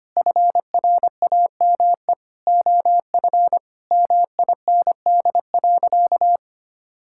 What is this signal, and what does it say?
Morse code, heard as a keyed tone, about 25 wpm, 700 Hz. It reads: FRAME OF MIND.
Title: Sentences from Top 1000 Words - Review Part 6 - 25wpm